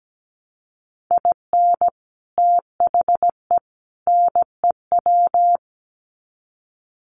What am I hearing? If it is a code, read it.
Morse code, heard as a keyed tone, about 17 wpm, 700 Hz. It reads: IN THE NEW
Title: Sets of 3 Words 17wpm